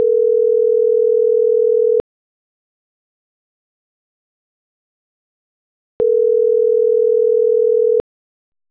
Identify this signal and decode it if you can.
Ringback tone (North America), 440 + 480 Hz.